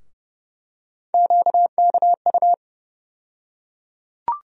Morse code, heard as a keyed tone, about 31 wpm, 700 Hz. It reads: QKU
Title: USA Call Sign Postfixes - 3 Letters 30wpm